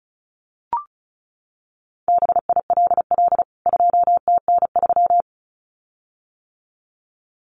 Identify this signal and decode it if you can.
Morse code, heard as a keyed tone, about 34 wpm, 700 Hz. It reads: BILL 2TN3
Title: CWT Exchange 35wpm